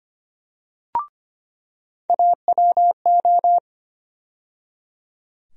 Morse code, heard as a keyed tone, about 25 wpm, 700 Hz. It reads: AWO
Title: USA Call Sign Postfixes - 3 Letters 25wpm